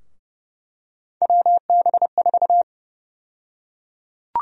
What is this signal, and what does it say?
Morse code, heard as a keyed tone, about 30 wpm, 700 Hz. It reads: WB4
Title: USA Call Sign Prefixes - 30wpm